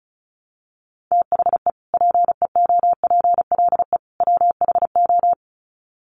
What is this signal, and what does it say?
Morse code, heard as a keyed tone, about 34 wpm, 700 Hz. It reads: THE PEOPLE WHO
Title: Sets of 3 Words 35wpm